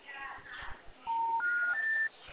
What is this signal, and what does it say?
Special information tone, with noise.